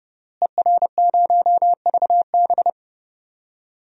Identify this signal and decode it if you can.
Morse code, heard as a keyed tone, about 30 wpm, 700 Hz. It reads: ER0VB
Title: DXCC-100 Call Signs - 5 Characters - Repeated 3x - 30wpm